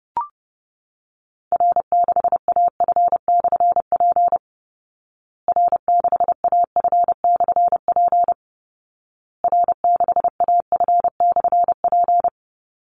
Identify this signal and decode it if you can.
Morse code, heard as a keyed tone, about 30 wpm, 700 Hz. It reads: R6AF/P R6AF/P R6AF/P
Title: Call Signs Repeated 2x - 30wpm